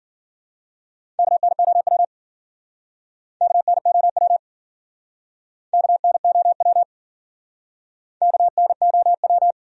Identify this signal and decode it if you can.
Morse code, heard as a keyed tone, about 52 wpm, 700 Hz. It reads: KNOW KNOW KNOW KNOW